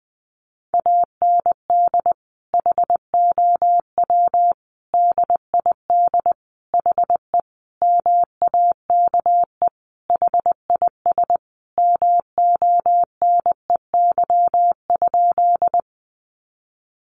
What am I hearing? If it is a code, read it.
Morse code, heard as a keyed tone, about 20 wpm, 700 Hz. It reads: AND HOW DID HE MAKE HIS MONEY?
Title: Sentences from Top 400 Words 20wpm